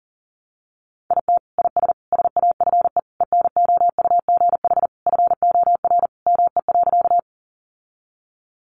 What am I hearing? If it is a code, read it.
Morse code, heard as a keyed tone, about 40 wpm, 700 Hz. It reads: IT IS SAFE ENOUGH FOR ME.
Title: Sentences from Top 900 Words 40wpm